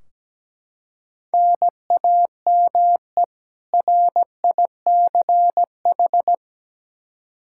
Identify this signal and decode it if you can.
Morse code, heard as a keyed tone, about 17 wpm, 700 Hz. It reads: NAME RICH
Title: K1USN SST QSO Practice 17wpm